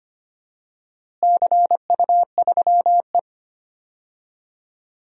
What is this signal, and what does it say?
Morse code, heard as a keyed tone, about 25 wpm, 700 Hz. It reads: CU3E
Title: DXCC-100 Call Signs - 4 Characters - Repeated 3x - 25wpm